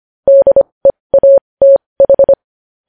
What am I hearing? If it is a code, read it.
Morse code, heard as a keyed tone, about 25 wpm, spaced as for 20 wpm, 550 Hz. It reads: DEATH
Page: news headlines in morse code at 20 wpm